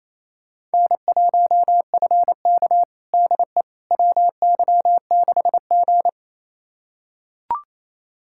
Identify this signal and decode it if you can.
Morse code, heard as a keyed tone, about 28 wpm, 700 Hz. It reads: N1FK DE WY6G
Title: USA Callsign DE Callsign - 4 Characters - 28wpm